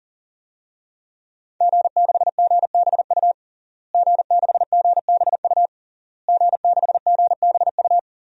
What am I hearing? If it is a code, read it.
Morse code, heard as a keyed tone, about 40 wpm, 700 Hz. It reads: G6GBU G6GBU G6GBU
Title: DXCC-100 Call Signs - 5 Characters - Repeated 3x - 40wpm